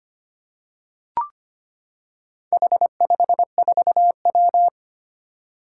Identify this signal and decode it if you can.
Morse code, heard as a keyed tone, about 25 wpm, 700 Hz. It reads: H54W
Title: Four Challenging Letter-Number Combinations 25wpm